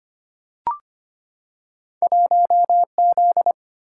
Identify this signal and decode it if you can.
Morse code, heard as a keyed tone, about 25 wpm, 700 Hz. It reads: 1Z
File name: Two Letter-Number Combinations - 25wpm